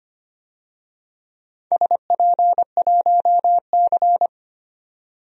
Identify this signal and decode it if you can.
Morse code, heard as a keyed tone, about 25 wpm, 700 Hz. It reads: SP1C
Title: DXCC-100 Call Signs - 4 Characters - 25wpm